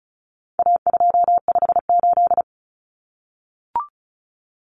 Morse code, heard as a keyed tone, about 34 wpm, 700 Hz. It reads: A258
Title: Common Cut Numbers - 4-Digit 35wpm